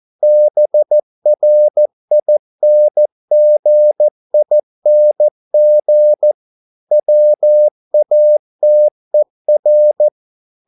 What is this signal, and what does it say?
Morse code, heard as a keyed tone, about 14 wpm, 600 Hz. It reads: BRINGING WATER